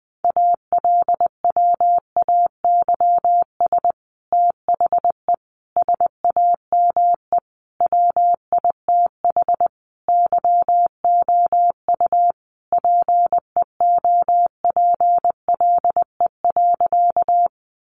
Morse code, heard as a keyed tone, about 19 wpm, 700 Hz. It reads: ALWAYS THE SAME WITH YOU PEOPLE.
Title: Sentences from Top 300 Words 20wpm